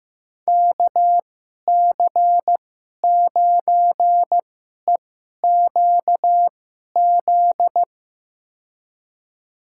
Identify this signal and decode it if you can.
Morse code, heard as a keyed tone, about 15 wpm, spaced as for 11 wpm, 700 Hz. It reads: KC9EQZ
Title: USA Call Signs - 6 Characters - 2x Character Spacing - 15wpm